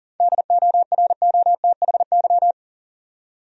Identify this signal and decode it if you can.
Morse code, heard as a keyed tone, about 40 wpm, 700 Hz. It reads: DOROTHY
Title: USA Common Names - 40wpm